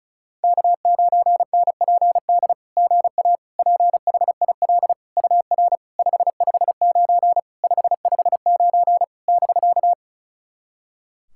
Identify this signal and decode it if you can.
Morse code, heard as a keyed tone, about 35 wpm, 700 Hz. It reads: K9NPD GA PHIL UR 559 559 <BK>
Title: SOTA QSO Practice 35wpm